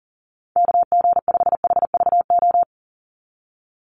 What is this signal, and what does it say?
Morse code, heard as a keyed tone, about 40 wpm, 700 Hz. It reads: KG5HVO
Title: Call Signs - 40wpm